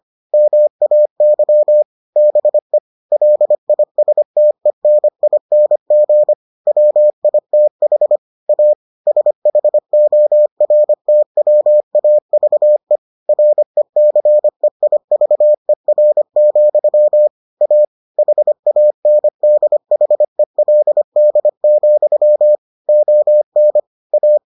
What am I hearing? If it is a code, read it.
Morse code, heard as a keyed tone, about 25 wpm, 600 Hz. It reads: MAY BE LISTENING WITH A SHORTWAVE RECEIVER, A HANDHELD, ON A